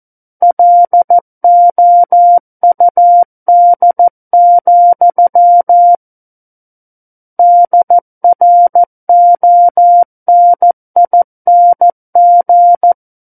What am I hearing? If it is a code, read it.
Morse code, heard as a keyed tone, about 14 wpm, 700 Hz. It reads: LOUD, DRONING
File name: Book One / 05-Chapter 5 - 14 WPM